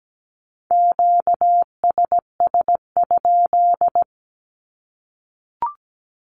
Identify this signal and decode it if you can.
Morse code, heard as a keyed tone, about 17 wpm, 700 Hz. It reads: QSS?
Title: Q-Codes Repeated 3x 17wpm